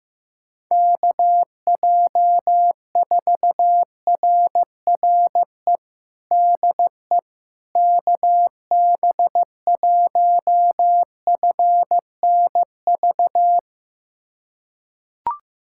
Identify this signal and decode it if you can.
Morse code, heard as a keyed tone, about 15 wpm, 700 Hz. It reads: KJ4RRE DE KB1FNV
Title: USA Callsign DE Callsign - 6 Characters - 15wpm